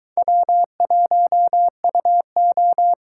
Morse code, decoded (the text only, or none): W1UO